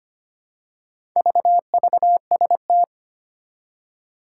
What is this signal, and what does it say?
Morse code, heard as a keyed tone, about 25 wpm, 700 Hz. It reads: VVST
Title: Four Challenging Letter-Number Combinations 25wpm